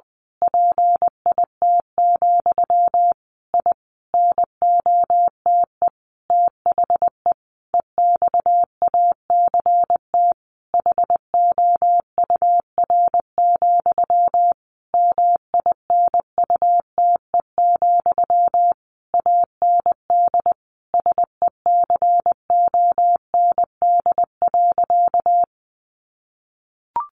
Morse code, heard as a keyed tone, about 20 wpm, 700 Hz. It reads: PIT, I NOTE THE EXACT HOUR, MINUTE, AND SECOND.
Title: A Sound Of Thunder 20wpm